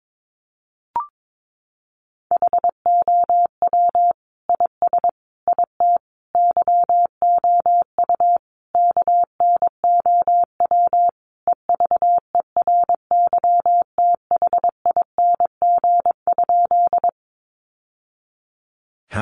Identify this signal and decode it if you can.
Morse code, heard as a keyed tone, about 22 wpm, 700 Hz. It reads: HOW IS IT YOU KNOW EVERYTHING?
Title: Sentences from Top 700 Words 22wpm